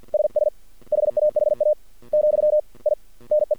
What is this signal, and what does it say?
Morse code, heard as a keyed tone, about 48 wpm, 600 Hz. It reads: SS HAHA , I NE
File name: archived / CRSnet